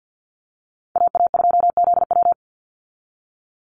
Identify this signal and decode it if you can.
Morse code, heard as a keyed tone, about 48 wpm, 700 Hz. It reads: AA2ZW
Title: Call Signs Repeated 2x - 50wpm